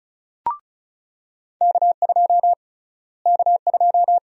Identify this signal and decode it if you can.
Morse code, heard as a keyed tone, about 36 wpm, 700 Hz. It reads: K2 K2
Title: USA Call Sign Prefixes - Repeated 2x - 35wpm